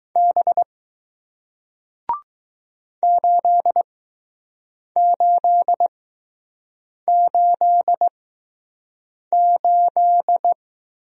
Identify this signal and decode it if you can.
Morse code, heard as a keyed tone, about 20 wpm, spaced as for 6 wpm, 700 Hz. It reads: B 8888